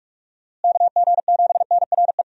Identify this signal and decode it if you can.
Morse code, heard as a keyed tone, about 43 wpm, 700 Hz. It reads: KG7NRE
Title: USA Call Signs - 6 Characters - Repeated 3x - 45wpm